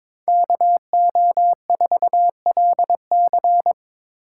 Morse code, heard as a keyed tone, about 25 wpm, 700 Hz. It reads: KO4LC
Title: Speed Racing - USA 2x2 Call Signs 22wpm